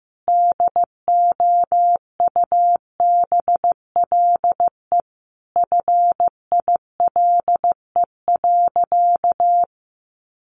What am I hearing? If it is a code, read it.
Morse code, heard as a keyed tone, about 15 wpm, 700 Hz. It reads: DOUBLE FILE.
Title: Sentences from Top 1400 Words 15wpm